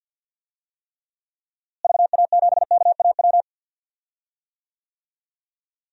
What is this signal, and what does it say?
Morse code, heard as a keyed tone, about 50 wpm, 700 Hz. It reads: VA7KAW
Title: DXCC-100 Call Signs - 6 Characters - Repeated 2x - 50wpm